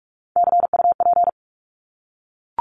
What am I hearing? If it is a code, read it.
Morse code, heard as a keyed tone, about 44 wpm, 700 Hz. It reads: CUP
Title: Top 1501-1750 Words 45wpm